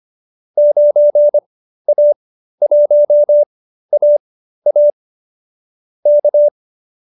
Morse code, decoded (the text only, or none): 9A1AA K